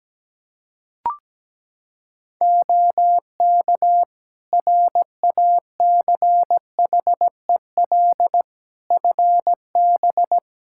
Morse code, heard as a keyed tone, about 17 wpm, 700 Hz. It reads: OK RACHEL FB
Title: Practice QSO - Exchange 2 - Info - 17wpm